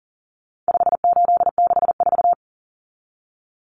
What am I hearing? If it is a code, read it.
Morse code, heard as a keyed tone, about 37 wpm, 700 Hz. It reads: 5864
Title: Numbers - 4-Digit 40wpm